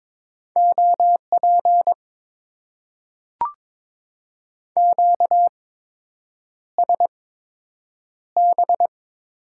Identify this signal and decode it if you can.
Morse code, heard as a keyed tone, about 22 wpm, spaced as for 8 wpm, 700 Hz. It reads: OP QSB
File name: Top 90 QSO Elements - 8x Character Spacing - 22wpm